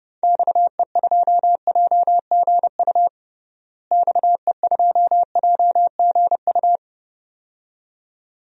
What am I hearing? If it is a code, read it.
Morse code, heard as a keyed tone, about 31 wpm, 700 Hz. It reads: XE2JGU XE2JGU
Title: DXCC-100 Call Signs - 6 Characters - Repeated 2x - 30wpm